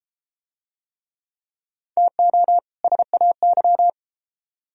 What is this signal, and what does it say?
Morse code, heard as a keyed tone, about 33 wpm, 700 Hz. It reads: TO SAY